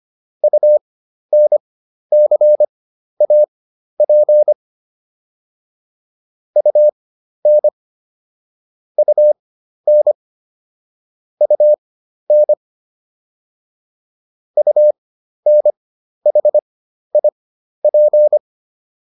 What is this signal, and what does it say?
Morse code, heard as a keyed tone, about 25 wpm, spaced as for 12 wpm, 600 Hz. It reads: UNCAP UN UN UN UNHIP